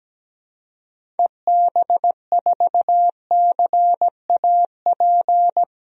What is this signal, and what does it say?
Morse code, heard as a keyed tone, about 17 wpm, 700 Hz. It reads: EB4CAP
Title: DXCC-100 Call Signs - 6 Characters - 17wpm